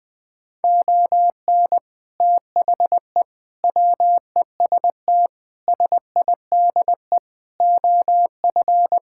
Morse code, heard as a keyed tone, about 20 wpm, 700 Hz. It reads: ON THE WEST SIDE OF
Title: Sets of 6 Words - 20wpm